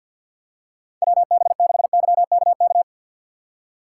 Morse code, heard as a keyed tone, about 52 wpm, 700 Hz. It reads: WB6YKK
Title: USA Call Signs - 4x Character Spacing - 50wpm